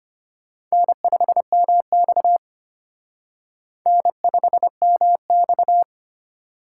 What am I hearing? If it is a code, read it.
Morse code, heard as a keyed tone, about 31 wpm, 700 Hz. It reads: N5MX N5MX
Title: Speed Racing - USA 1x2 Call Signs 25wpm